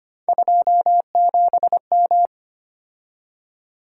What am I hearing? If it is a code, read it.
Morse code, heard as a keyed tone, about 25 wpm, 700 Hz. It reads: 27M
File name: Three Letter-Number Combinations - 25wpm